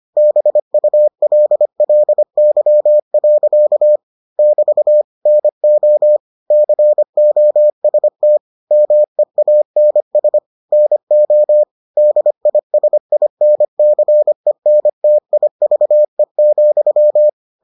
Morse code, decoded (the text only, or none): BULLY. <BT> NO COST MEANS NO DISINCENTIVE,